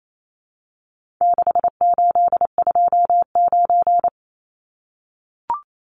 Morse code, heard as a keyed tone, about 28 wpm, 700 Hz. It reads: 6829
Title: Numbers - 4-Digit 28wpm